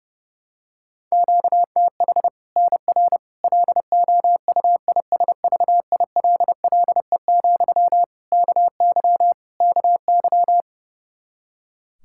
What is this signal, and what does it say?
Morse code, heard as a keyed tone, about 32 wpm, 700 Hz. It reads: QTH NR LOUISVILLE, KY KY
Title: K1USN SST QSO Practice 30wpm